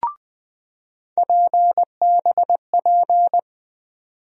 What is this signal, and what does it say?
Morse code, heard as a keyed tone, about 21 wpm, 700 Hz. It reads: PBP
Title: USA Call Sign Postfixes - 3 Letters 20wpm